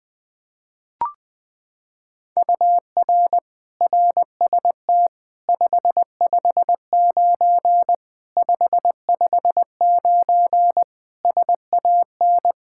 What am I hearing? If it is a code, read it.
Morse code, heard as a keyed tone, about 20 wpm, 700 Hz. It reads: UR RST 559 559 SAN
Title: Practice QSO - Exchange 1 - Beginning - 20wpm